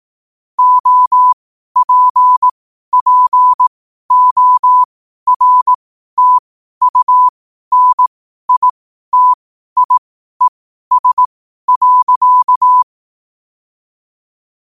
Morse code, heard as a keyed tone, about 18 wpm, spaced as for 12 wpm, 1000 Hz. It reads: OPPORTUNITIES.
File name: qotdfiles / 12wpm